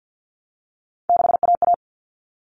Morse code, heard as a keyed tone, about 48 wpm, 700 Hz. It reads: 6AA
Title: Common Cut Numbers - 3-Digit 50wpm